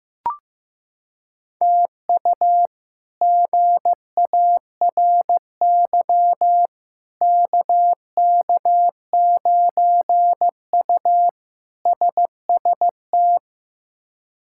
Morse code, heard as a keyed tone, about 15 wpm, 700 Hz. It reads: TU GARY KK9U SST
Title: K1USN SST QSO Practice - 15wpm